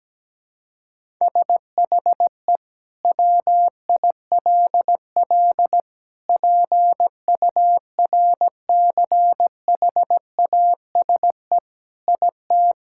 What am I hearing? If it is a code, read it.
Morse code, heard as a keyed tone, about 17 wpm, 700 Hz. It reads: SHE WILL PURCHASE IT
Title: Sentences from Top 1000 Words - Review Part 5 - 17wpm